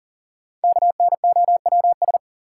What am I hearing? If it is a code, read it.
Morse code, heard as a keyed tone, about 40 wpm, 700 Hz. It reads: KNOWS
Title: Sentences from Top 1000 Words - Review Part 1 - New Word Drill - 40wpm